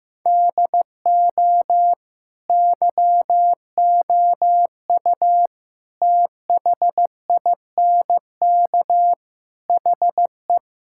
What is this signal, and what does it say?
Morse code, heard as a keyed tone, about 15 wpm, 700 Hz. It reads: DO YOU THINK HE
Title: Sentences from Top 1600 Words 15wpm